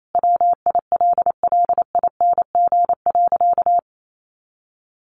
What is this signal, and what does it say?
Morse code, heard as a keyed tone, about 28 wpm, 700 Hz. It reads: WILLING.
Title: Sentences from Top 1400 Words 28wpm